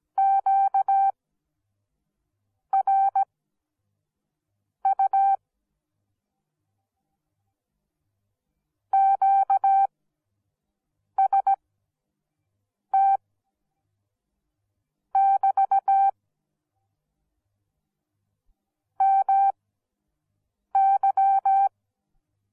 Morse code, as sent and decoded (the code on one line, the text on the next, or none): --.- .-. ..- / --.- ... - / -...- / -- -.--
QRU QST <BT> MY